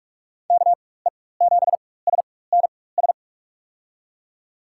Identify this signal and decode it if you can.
Morse code, heard as a keyed tone, about 47 wpm, spaced as for 21 wpm, 700 Hz. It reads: KE7SNS
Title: USA Call Signs - 6 Characters - 4x Character Spacing - 45wpm